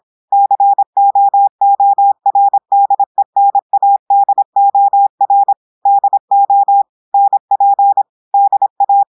Morse code, heard as a keyed tone, about 26 wpm, 800 Hz. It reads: COORDENADOR DO NP DA
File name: audios / portugais / portugais texte en clair 1 vitesse 26 mots minutes